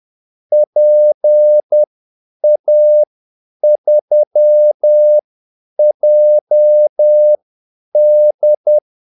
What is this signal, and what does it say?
Morse code, heard as a keyed tone, about 10 wpm, spaced as for 8 wpm, 600 Hz. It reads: PA3JD
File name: Callsigns-10-8wpm